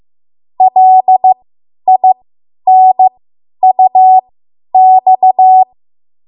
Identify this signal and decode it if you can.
Morse code, heard as a keyed tone, about 15 wpm, spaced as for 10 wpm, 750 Hz. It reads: LINUX
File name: Morse / 10WPM